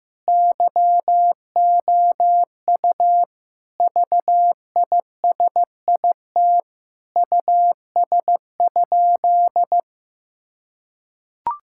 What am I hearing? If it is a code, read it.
Morse code, heard as a keyed tone, about 15 wpm, 700 Hz. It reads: YOU VISIT US?
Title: Sentences from Top 1000 Words - Review Part 6 - 15wpm